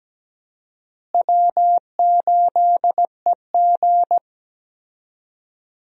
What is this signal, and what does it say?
Morse code, heard as a keyed tone, about 17 wpm, 700 Hz. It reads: W8EG
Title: USA Call Signs - 4 Characters - Repeated 3x - 17wpm